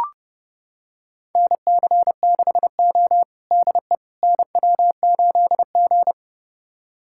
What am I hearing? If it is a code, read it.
Morse code, heard as a keyed tone, about 30 wpm, 700 Hz. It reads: NC6O DE NW8G
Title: USA Callsign DE Callsign - 4 Characters - 30wpm